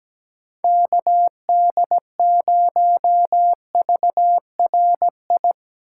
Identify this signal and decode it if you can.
Morse code, heard as a keyed tone, about 17 wpm, 700 Hz. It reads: KD0VRI
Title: USA Call Signs - 6 Characters - 2x Character Spacing - 17wpm